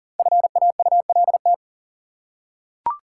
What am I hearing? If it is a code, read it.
Morse code, heard as a keyed tone, about 40 wpm, 700 Hz. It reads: FAULT
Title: Five Letter Words - 40wpm